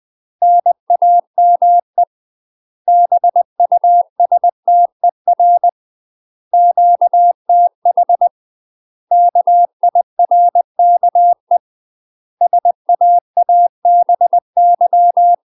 Morse code, heard as a keyed tone, about 20 wpm, 700 Hz. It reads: NAME BUSTER QTH KIRKE SAABY
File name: UK MASTER QSOs TRAINING/20WPM